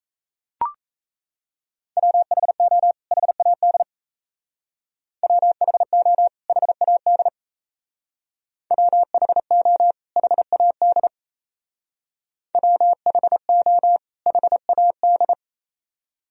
Morse code, heard as a keyed tone, about 36 wpm, 700 Hz. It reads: WHO HAD WHO HAD WHO HAD WHO HAD